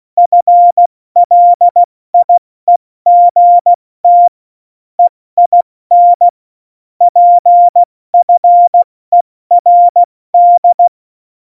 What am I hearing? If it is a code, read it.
Morse code, heard as a keyed tone, about 16 wpm, 700 Hz. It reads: FLIEGT EIN PFERD
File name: Morsekurs / Witze / MP3, Witz 08. Mai 2018, 70 F80